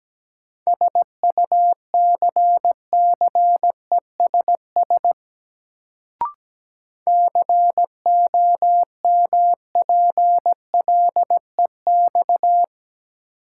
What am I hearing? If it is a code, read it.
Morse code, heard as a keyed tone, about 17 wpm, 700 Hz. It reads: SUCCESS COMPLEX
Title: Sentences from Top 900 Words - New Word Drill - 17wpm